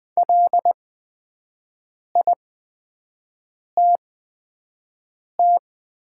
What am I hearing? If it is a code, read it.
Morse code, heard as a keyed tone, about 20 wpm, spaced as for 5 wpm, 700 Hz. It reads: LITT
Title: 8x Character Spacing - Top 200 Words - 20wpm